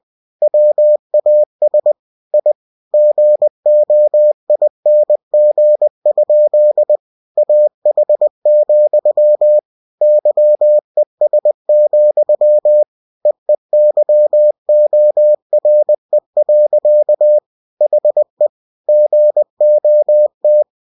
Morse code, heard as a keyed tone, about 20 wpm, 600 Hz. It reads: WAS I GOING? AH, YES, EEYORE. HE GOT